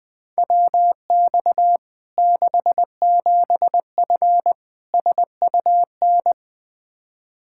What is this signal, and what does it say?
Morse code, heard as a keyed tone, about 20 wpm, 700 Hz. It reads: WX 67F SUN
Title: K1USN SST QSO Practice 20wpm